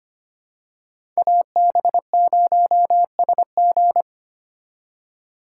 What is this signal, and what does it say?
Morse code, heard as a keyed tone, about 25 wpm, 700 Hz. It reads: AB0SG
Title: USA Call Signs - 5 Characters - 4x Character Spacing - 25wpm